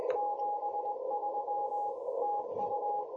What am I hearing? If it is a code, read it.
Morse code, heard as a keyed tone, about 20 wpm, 830 Hz. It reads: 73 TU
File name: jh8cbh / 2021cw